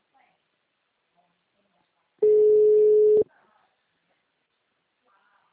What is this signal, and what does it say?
Ringback tone (Europe, 425 Hz), with speech.